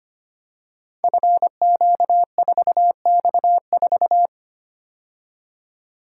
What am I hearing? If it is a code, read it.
Morse code, heard as a keyed tone, about 25 wpm, 700 Hz. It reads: FQ4X4
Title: Code Groups - 5 Letter:Numbers - 25wpm